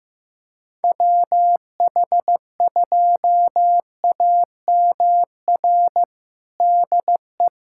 Morse code, heard as a keyed tone, about 15 wpm, 700 Hz. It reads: WH2AMR DE
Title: USA Callsign DE Callsign - 6 Characters - 15wpm